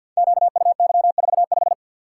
Morse code, heard as a keyed tone, about 48 wpm, 700 Hz. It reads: QUY45
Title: Code Groups - 5 Letter:Numbers - 50wpm